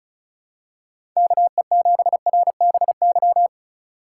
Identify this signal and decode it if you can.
Morse code, heard as a keyed tone, about 34 wpm, 700 Hz. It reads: KE7RBY
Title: USA Call Signs - 4x Character Spacing - 35wpm